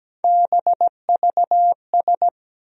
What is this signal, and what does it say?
Morse code, heard as a keyed tone, about 17 wpm, 700 Hz. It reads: BVS
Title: USA Call Sign Postfixes - 3 Letters 17wpm